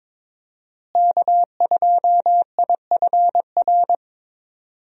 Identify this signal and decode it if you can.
Morse code, heard as a keyed tone, about 22 wpm, 700 Hz. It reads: K2IFR
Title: USA Call Signs - 5 Characters - Repeated 3x - 22wpm